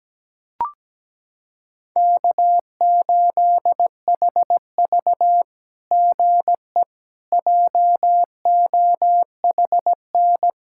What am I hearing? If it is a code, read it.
Morse code, heard as a keyed tone, about 17 wpm, 700 Hz. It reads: K8HV GE JOHN
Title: SOTA QSO Practice 17wpm